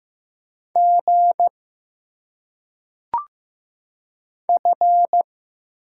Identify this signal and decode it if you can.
Morse code, heard as a keyed tone, about 15 wpm, 700 Hz. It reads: G F
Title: Single Letters - Phonetic Pronunciation 15wpm